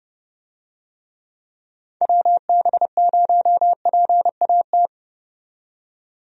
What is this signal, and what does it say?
Morse code, heard as a keyed tone, about 30 wpm, 700 Hz. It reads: WB0PAT
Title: USA Call Signs - 6 Characters - 30wpm